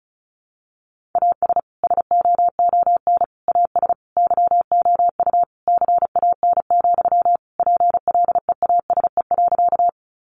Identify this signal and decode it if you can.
Morse code, heard as a keyed tone, about 34 wpm, 700 Hz. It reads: AS SOON AS YOU CAN, PLEASE.
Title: Sentences from Top 1500 Words 35wpm